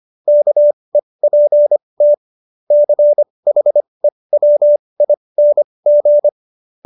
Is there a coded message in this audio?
Yes: Morse code from 0.3 to 6.3 s, heard as a keyed tone, about 25 wpm, spaced as for 20 wpm, 590 Hz. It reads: KEPT CHEWING